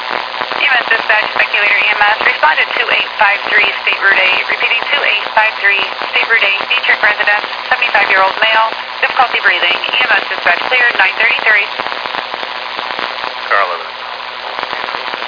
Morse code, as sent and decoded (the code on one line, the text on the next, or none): none